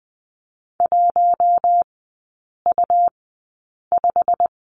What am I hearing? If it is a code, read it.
Morse code, heard as a keyed tone, about 20 wpm, spaced as for 8 wpm, 700 Hz. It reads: 1U5
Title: 138 Lesson 23e - Letter f - Three Characters 20wpm